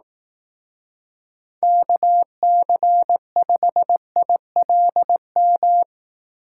Morse code, heard as a keyed tone, about 18 wpm, 700 Hz. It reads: KC5ILM
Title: Speed Racing - USA 2x3 Call Signs 15wpm